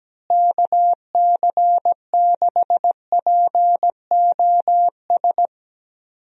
Morse code, heard as a keyed tone, about 17 wpm, 700 Hz. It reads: KC6POS
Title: USA Call Signs - 6 Characters - Repeated 2x - 17wpm